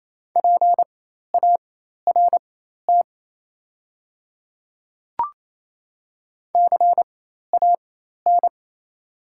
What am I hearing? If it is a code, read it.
Morse code, heard as a keyed tone, about 28 wpm, spaced as for 13 wpm, 690 Hz. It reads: PART CAN
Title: 4x Character Spacing - Top 200 Words - 28wpm